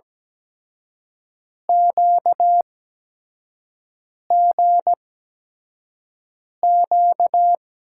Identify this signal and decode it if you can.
Morse code, heard as a keyed tone, about 17 wpm, spaced as for 5 wpm, 700 Hz. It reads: QGQ
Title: USA Call Signs - 6 Characters - 8x Character Spacing - 17wpm